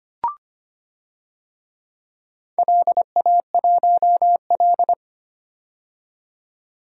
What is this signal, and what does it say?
Morse code, heard as a keyed tone, about 25 wpm, 700 Hz. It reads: LA1L
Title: DXCC-100 Call Signs - 4 Characters - 25wpm